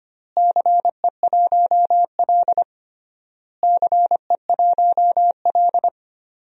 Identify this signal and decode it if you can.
Morse code, heard as a keyed tone, about 25 wpm, 700 Hz. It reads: CE1L CE1L
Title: DXCC-100 Call Signs - 4 Characters - Repeated 3x - 25wpm